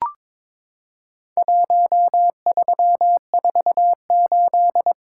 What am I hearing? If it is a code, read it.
Morse code, heard as a keyed tone, about 22 wpm, 700 Hz. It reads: 1348
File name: practice / 22wpm / Four Letter-Number Combinations - 22wpm